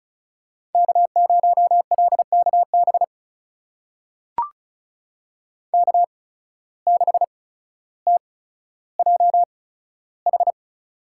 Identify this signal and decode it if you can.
Morse code, heard as a keyed tone, about 36 wpm, spaced as for 15 wpm, 700 Hz. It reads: K0LKB K6TJH
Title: USA Call Signs - 8x Character Spacing - 35wpm